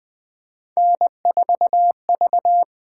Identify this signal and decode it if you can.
Morse code, heard as a keyed tone, about 20 wpm, 700 Hz. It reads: N4V
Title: USA Call Signs - 4 Characters - 8x Character Spacing - 20wpm